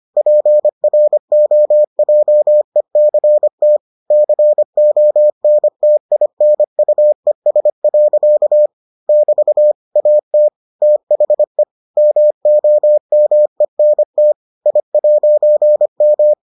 Morse code, heard as a keyed tone, about 25 wpm, 600 Hz. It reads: PROJECT CONTINUES. <BT> AT THE MOMENT I'M